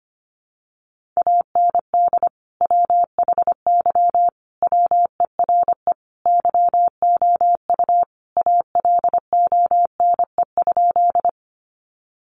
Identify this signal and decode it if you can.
Morse code, heard as a keyed tone, about 25 wpm, 700 Hz. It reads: AND WHY WERE YOU ALONE?
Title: Sentences from Top 1000 Words 25wpm